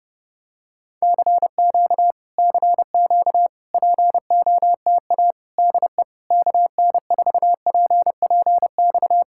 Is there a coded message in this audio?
Yes: Morse code from 1.0 to 9.2 s, heard as a keyed tone, about 30 wpm, 700 Hz. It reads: CQ CQ POTA DE KN4PPX